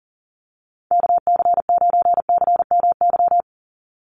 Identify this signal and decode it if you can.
Morse code, heard as a keyed tone, about 40 wpm, 700 Hz. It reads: KC9CMY
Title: USA Call Signs 40wpm